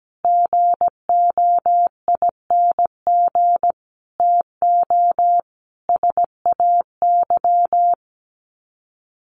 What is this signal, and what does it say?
Morse code, heard as a keyed tone, about 17 wpm, 700 Hz. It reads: GOING TO SAY
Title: Sets of 5 Words 17wpm